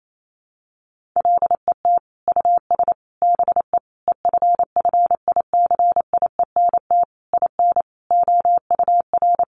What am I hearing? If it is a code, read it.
Morse code, heard as a keyed tone, about 28 wpm, 700 Hz. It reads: LET US BE EFFICIENT IN OUR
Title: Sentences from Top 1300 Words 28wpm